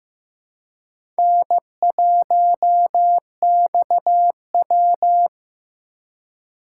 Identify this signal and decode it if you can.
Morse code, heard as a keyed tone, about 15 wpm, 700 Hz. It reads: N1XW
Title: USA Call Signs - 4 Characters - 4x Character Spacing - 15wpm